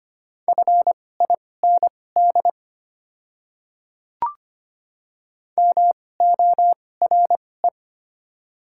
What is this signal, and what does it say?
Morse code, heard as a keyed tone, about 25 wpm, spaced as for 18 wpm, 700 Hz. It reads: FIND MORE
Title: 2x Character Spacing - Top 100 Words - 25wpm